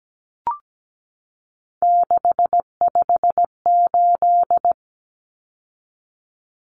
Morse code, heard as a keyed tone, about 17 wpm, 700 Hz. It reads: 658
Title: Numbers - 3-Digit 17wpm